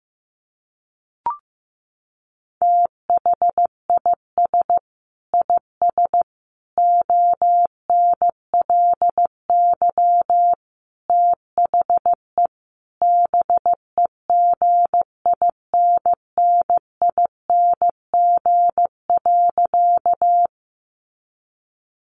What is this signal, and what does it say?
Morse code, heard as a keyed tone, about 15 wpm, 700 Hz. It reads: THIS IS ONLY THE BEGINNING.
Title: Sentences from Top 800 Words 15wpm